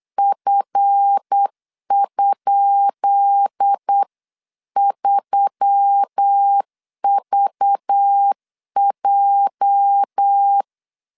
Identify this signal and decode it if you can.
Morse code, heard as a keyed tone, about 8 wpm, 800 Hz. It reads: F?3VJ
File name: LEZIONI / 10 SIGNS / 10B INT 8wpm 3534